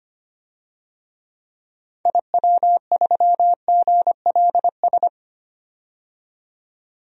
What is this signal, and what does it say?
Morse code, heard as a keyed tone, about 25 wpm, 700 Hz. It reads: IW3GLS
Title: DXCC-100 Call Signs - 6 Characters - 25wpm